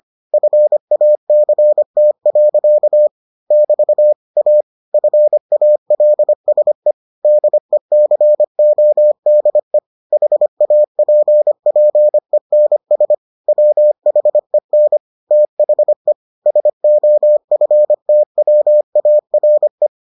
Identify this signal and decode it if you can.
Morse code, heard as a keyed tone, about 25 wpm, 600 Hz. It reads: FACT. <BT> A FALSE DECODE HAPPENS WHEN THE SOFTWARE